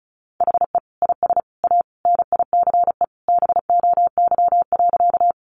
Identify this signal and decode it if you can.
Morse code, heard as a keyed tone, about 34 wpm, 700 Hz. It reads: HE IS A NICE BOY.
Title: Sentences from Top 1100 Words 35wpm